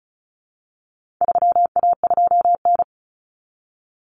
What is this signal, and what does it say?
Morse code, heard as a keyed tone, about 34 wpm, 700 Hz. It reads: 3A2N
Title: Common Cut Numbers - 4-Digit 35wpm